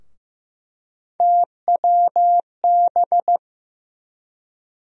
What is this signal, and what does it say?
Morse code, heard as a keyed tone, about 15 wpm, 700 Hz. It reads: TWB
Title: USA Call Sign Postfixes - 3 Letters 15wpm